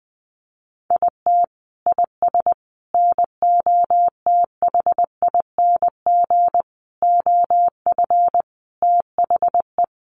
Morse code, heard as a keyed tone, about 20 wpm, 700 Hz. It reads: IT IS NOTHING OF THE
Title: Sentences from Top 1300 Words 20wpm